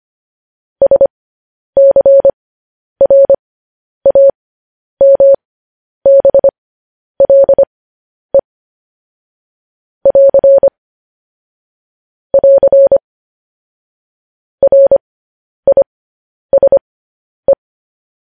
Morse code, heard as a keyed tone, about 25 wpm, spaced as for 10 wpm, 550 Hz. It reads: SCRAMBLE <AR> <AR> RISE
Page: news headlines in morse code at 10 wpm